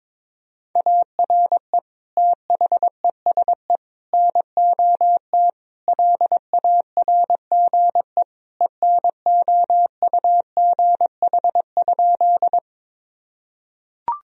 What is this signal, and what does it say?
Morse code, heard as a keyed tone, about 22 wpm, 700 Hz. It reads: ARE THESE NOT LARGE ENOUGH?
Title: Sentences from Top 1000 Words - Review Part 1 - 22wpm